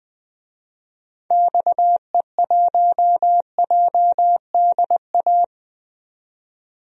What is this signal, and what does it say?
Morse code, heard as a keyed tone, about 21 wpm, 700 Hz. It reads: XE1JDA
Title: DXCC-100 Call Signs - 6 Characters - 20wpm